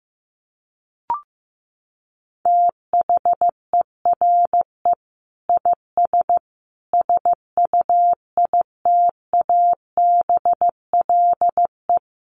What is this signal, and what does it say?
Morse code, heard as a keyed tone, about 15 wpm, 700 Hz. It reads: THERE IS SUITABLE
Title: Sentences from Top 1500 Words 15wpm